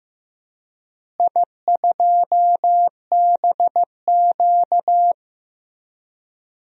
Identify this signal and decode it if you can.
Morse code, heard as a keyed tone, about 16 wpm, 700 Hz. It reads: I2BQ
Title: DXCC-100 Call Signs - 4 Characters - Repeated 3x - 15wpm